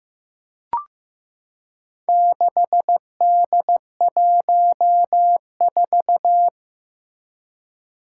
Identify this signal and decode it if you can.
Morse code, heard as a keyed tone, about 16 wpm, 700 Hz. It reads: 6D14
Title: Four Challenging Letter-Number Combinations 15wpm